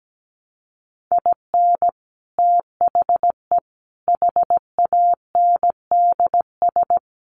Morse code, heard as a keyed tone, about 17 wpm, 700 Hz. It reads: IN THE HANDS
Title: Sets of 4 Words 17wpm